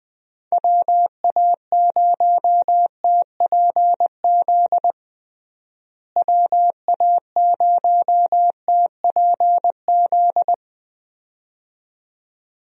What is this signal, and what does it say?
Morse code, heard as a keyed tone, about 21 wpm, 700 Hz. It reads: WA0TPZ WA0TPZ
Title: USA Call Signs - 6 Characters - Repeated 2x - 20wpm